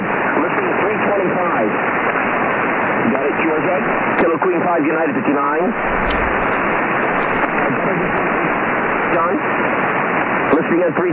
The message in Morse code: none